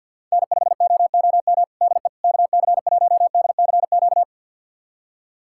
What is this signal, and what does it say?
Morse code, heard as a keyed tone, about 52 wpm, 700 Hz. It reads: N5OOM DE KC1DCQ
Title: USA Callsign DE Callsign 50wpm